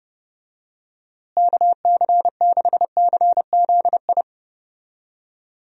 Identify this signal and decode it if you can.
Morse code, heard as a keyed tone, about 31 wpm, 700 Hz. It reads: KC6CZI
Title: USA Call Signs - 6 Characters - Repeated 2x - 30wpm